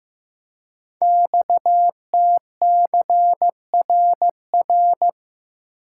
Morse code, heard as a keyed tone, about 15 wpm, 700 Hz. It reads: XTCRR
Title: Code Groups - 5 Letters - 15wpm